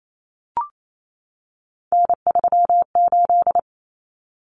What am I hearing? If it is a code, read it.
Morse code, heard as a keyed tone, about 28 wpm, 700 Hz. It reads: N38
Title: Common Cut Numbers - 3-Digit 28wpm